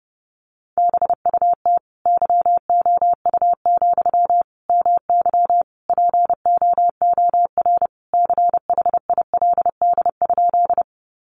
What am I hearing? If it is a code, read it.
Morse code, heard as a keyed tone, about 30 wpm, 700 Hz. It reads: BUT YOU, MY POOR CHILD?
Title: Sentences from Top 1000 Words 30wpm